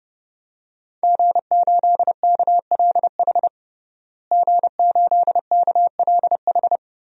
Morse code, heard as a keyed tone, about 29 wpm, 700 Hz. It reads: G8KLH G8KLH
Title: DXCC-100 Call Signs - 5 Characters - Repeated 3x - 30wpm